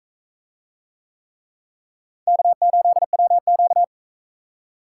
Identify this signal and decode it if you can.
Morse code, heard as a keyed tone, about 40 wpm, 700 Hz. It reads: K8WQ